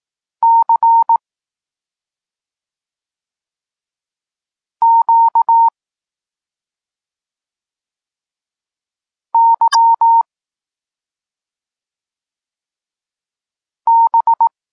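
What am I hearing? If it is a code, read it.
Morse code, heard as a keyed tone, about 18 wpm, 920 Hz. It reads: C Q Y B
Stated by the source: Polish Morse lesson teaching the letter C, from SP9MOA Niepołomicki Klub Krótkofalowców.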